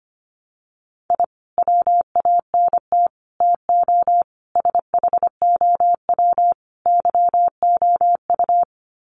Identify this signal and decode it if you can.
Morse code, heard as a keyed tone, about 25 wpm, 700 Hz. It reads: I WANT TO SHOW YOU
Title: Sets of 5 Words 25wpm